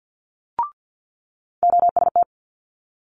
Morse code, heard as a keyed tone, about 50 wpm, 700 Hz. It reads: OST
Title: Multi-letter Phonemes Quick Fire 50wpm